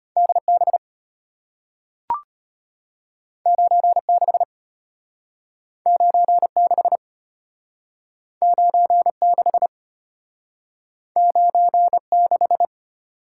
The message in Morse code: DB 96 96 96 96

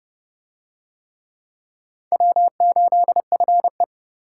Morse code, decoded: W8FE

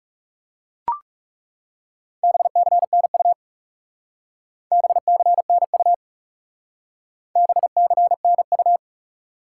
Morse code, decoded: BCNU BCNU BCNU

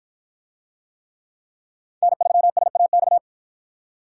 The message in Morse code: N3SAX

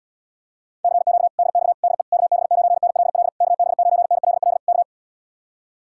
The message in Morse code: CQ DX DE KB9NLL KB9NLL K

